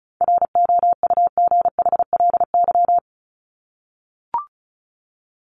Morse code, heard as a keyed tone, about 34 wpm, 700 Hz. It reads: ROUGHLY
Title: Top 1251-1500 Words 35wpm